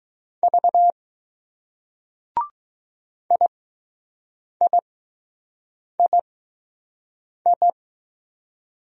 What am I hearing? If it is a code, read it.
Morse code, heard as a keyed tone, about 20 wpm, spaced as for 6 wpm, 700 Hz. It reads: V IIII